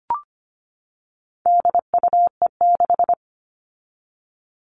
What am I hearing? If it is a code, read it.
Morse code, heard as a keyed tone, about 25 wpm, 700 Hz. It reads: DUE6
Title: All Cut Numbers - 4-Digit 25wpm